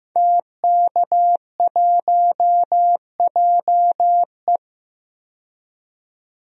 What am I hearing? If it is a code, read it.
Morse code, heard as a keyed tone, about 15 wpm, 700 Hz. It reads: TK1JE
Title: DXCC-100 Call Signs - 5 Characters - 15wpm